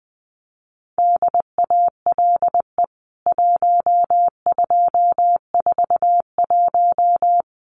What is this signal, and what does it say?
Morse code, heard as a keyed tone, about 19 wpm, 700 Hz. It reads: DALE 1241
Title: CWT Exchange 20wpm